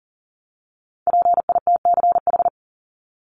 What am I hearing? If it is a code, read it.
Morse code, heard as a keyed tone, about 40 wpm, 700 Hz. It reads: PITCH
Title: Top 1251-1500 Words 40wpm